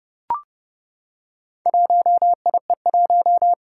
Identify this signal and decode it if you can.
Morse code, heard as a keyed tone, about 32 wpm, 700 Hz. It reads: 1IE1
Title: Four Challenging Letter-Number Combinations 30wpm